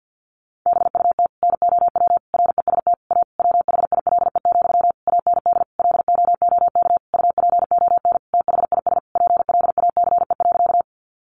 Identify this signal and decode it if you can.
Morse code, heard as a keyed tone, about 50 wpm, 700 Hz. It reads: BUT NOW REST A WHILE, AND LOOK UPON THIS PLACE.
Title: Sentences from Top 1100 Words 50wpm